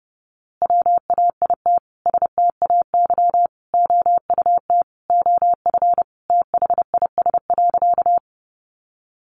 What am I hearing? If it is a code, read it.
Morse code, heard as a keyed tone, about 30 wpm, 700 Hz. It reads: WAIT STAY OUT OF THIS.
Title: A Sound Of Thunder 30wpm